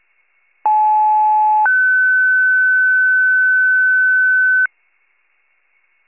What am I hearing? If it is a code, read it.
Two-tone page, heard as a one-second tone, then a three-second tone, about 860 then 1500 Hz. It is heard from 0.7 to 4.7 s.